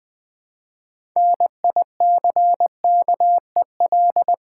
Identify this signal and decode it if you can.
Morse code, heard as a keyed tone, about 20 wpm, 700 Hz. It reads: NICKEL
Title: Build A Word - 3 Letters At A Time - 6-Letter Words - 20wpm